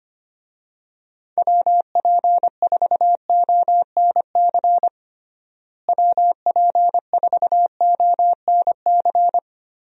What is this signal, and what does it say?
Morse code, heard as a keyed tone, about 25 wpm, 700 Hz. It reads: WP4ONC WP4ONC
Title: USA Call Signs - 6 Characters - Repeated 3x - 25wpm